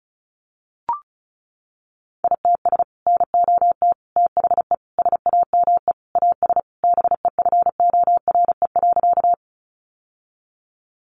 Morse code, heard as a keyed tone, about 34 wpm, 700 Hz. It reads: ITS NOT THE SAME AS BEFORE.
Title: Sentences from Top 600 Words 35wpm